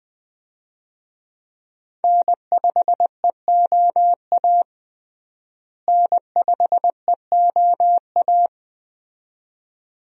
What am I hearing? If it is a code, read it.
Morse code, heard as a keyed tone, about 20 wpm, 700 Hz. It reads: N5EOA N5EOA
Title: USA Call Signs - 5 Characters - Repeated 2x - 20wpm